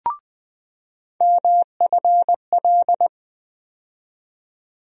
Morse code, heard as a keyed tone, about 20 wpm, 700 Hz. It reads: MFL
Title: USA Call Sign Postfixes - 3 Letters 20wpm